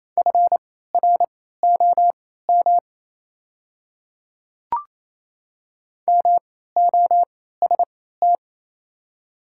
Morse code, heard as a keyed tone, about 28 wpm, spaced as for 16 wpm, 700 Hz. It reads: FROM MOST
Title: 3x Character Spacing - Top 100 Words - 28wpm